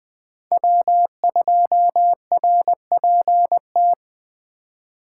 Morse code, heard as a keyed tone, about 20 wpm, 700 Hz. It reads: W2RPT
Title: Speed Racing - USA 1x3 Call Signs 17wpm